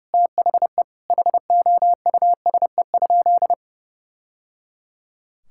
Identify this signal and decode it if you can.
Morse code, heard as a keyed tone, about 30 wpm, 700 Hz. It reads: THE HOUSE?
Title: Sentences from Top 1000 Words - Review Part 5 - 30wpm